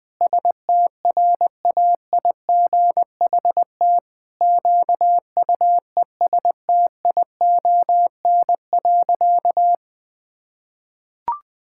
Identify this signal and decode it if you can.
Morse code, heard as a keyed tone, about 20 wpm, 700 Hz. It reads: STRAIGHT QUESTION.
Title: Sentences from Top 1000 Words - Review Part 6 - 20wpm